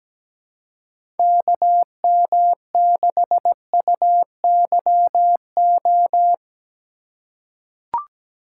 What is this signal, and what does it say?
Morse code, heard as a keyed tone, about 17 wpm, 700 Hz. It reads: KM6UYO
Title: USA Call Signs - 6 Characters - 4x Character Spacing - 17wpm